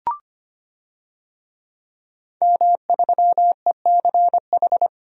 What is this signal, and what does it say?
Morse code, heard as a keyed tone, about 25 wpm, 700 Hz. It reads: M3ECH
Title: DXCC-100 Call Signs - 5 Characters - Repeated 2x - 25wpm